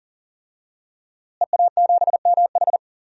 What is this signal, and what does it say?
Morse code, heard as a keyed tone, about 40 wpm, 700 Hz. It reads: EA7MH
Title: DXCC-100 Call Signs - 5 Characters - Repeated 2x - 40wpm